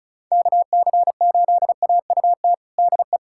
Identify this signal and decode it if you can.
Morse code, heard as a keyed tone, about 36 wpm, 700 Hz. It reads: KC8AUT DE
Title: USA Callsign DE Callsign - 6 Characters - 35wpm